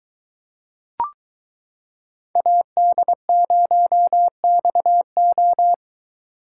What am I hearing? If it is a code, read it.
Morse code, heard as a keyed tone, about 23 wpm, 700 Hz. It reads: AD0XO